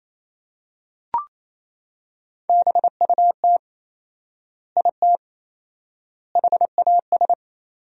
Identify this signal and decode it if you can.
Morse code, heard as a keyed tone, about 28 wpm, 700 Hz. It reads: BUT IT HAS
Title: 4x Word-Spacing - Sentences from Top 100 Words - 28wpm